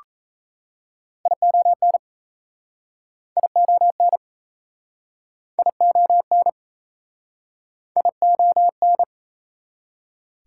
Morse code, heard as a keyed tone, about 39 wpm, 700 Hz. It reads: ION ION ION ION